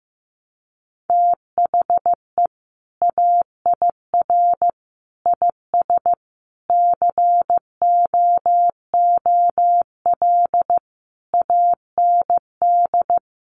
Morse code, heard as a keyed tone, about 14 wpm, 700 Hz. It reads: THE AIR IS COOL AND
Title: Sentences from Top 1200 Words 15wpm